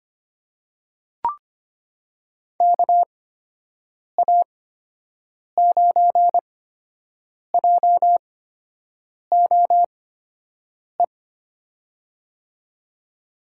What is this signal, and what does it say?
Morse code, heard as a keyed tone, about 25 wpm, spaced as for 7 wpm, 700 Hz. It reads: KA9JOE
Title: USA Call Signs - 8x Character Spacing - 25wpm